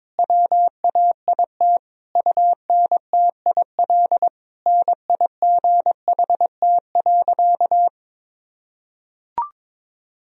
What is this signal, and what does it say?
Morse code, heard as a keyed tone, about 22 wpm, 700 Hz. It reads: WAIT UNTIL NIGHT.
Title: Sentences from Top 1000 Words - Review Part 1 - 22wpm